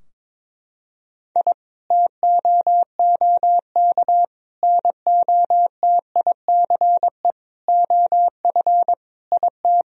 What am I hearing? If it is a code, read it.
Morse code, heard as a keyed tone, about 22 wpm, 700 Hz. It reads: I TOOK NOTICE OF IT
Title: Sentences from Top 1000 Words - Review Part 2 - 22wpm